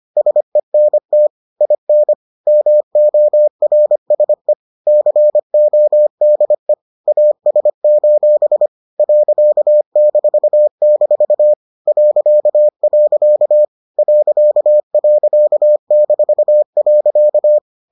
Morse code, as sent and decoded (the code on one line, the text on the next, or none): ... -. - / .. -. / -- --- .-. ... / -.-. --- -.. / .- ... ---... / .-.-.- -....- -....- / .-.-.- .-.-.- / .-.-.- .-.-.- -....- .-.-.-
SENT IN MORSE CODE AS: .-- .. ..-.